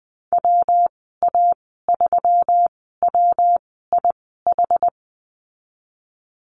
Morse code, heard as a keyed tone, about 20 wpm, spaced as for 15 wpm, 700 Hz. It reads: WA3WIH
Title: 179 Lesson 29g - Slash - Call Signs 20wpm